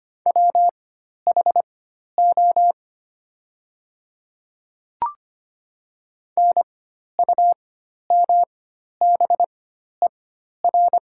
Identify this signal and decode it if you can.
Morse code, heard as a keyed tone, about 25 wpm, spaced as for 12 wpm, 700 Hz. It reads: WHO NUMBER
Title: 4x Character Spacing - Top 100 Words - 25wpm